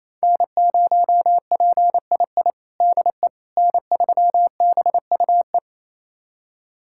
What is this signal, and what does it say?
Morse code, heard as a keyed tone, about 28 wpm, 700 Hz. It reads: N0PII DE N3BUE
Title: USA Callsign DE Callsign - 5 Characters - 28wpm